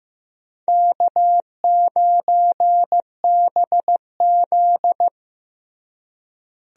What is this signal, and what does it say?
Morse code, heard as a keyed tone, about 15 wpm, 700 Hz. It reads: K9BZ
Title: USA Call Signs - 4 Characters - Repeated 3x - 15wpm